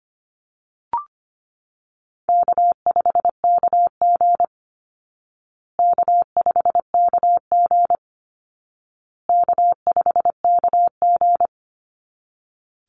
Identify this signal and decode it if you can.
Morse code, heard as a keyed tone, about 25 wpm, 700 Hz. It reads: K5KG K5KG K5KG
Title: Easy Call Signs Repeated 3x - 25wpm